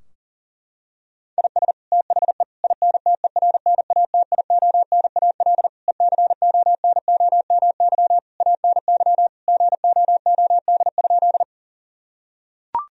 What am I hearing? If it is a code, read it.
Morse code, heard as a keyed tone, about 40 wpm, 700 Hz. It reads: IS THE INTERNATIONAL ECONOMY ANY GOOD?
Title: Sentences from Top 1000 Words - Review Part 4 - 40wpm